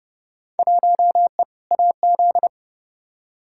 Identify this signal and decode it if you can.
Morse code, heard as a keyed tone, about 30 wpm, 700 Hz. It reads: 1E AZ
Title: Field Day Exchanges - 30wpm